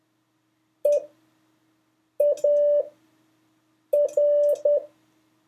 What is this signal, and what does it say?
Morse code, heard as a keyed tone, about 10 wpm, spaced as for 5 wpm, 580 Hz. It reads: EAR
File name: 3of5-words-2